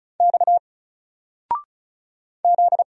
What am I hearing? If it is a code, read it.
Morse code, heard as a keyed tone, about 36 wpm, 700 Hz. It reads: X Z
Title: Single Letter-Number 35wpm